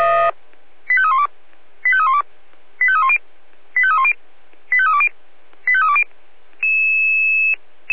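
Five-tone selective call (ZVEI), heard as quick runs of five tones, tones of about 80 ms. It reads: 85312 85312 85310 85310 85320 85320
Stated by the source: ZVEI five-tone call 85894, received on SDR.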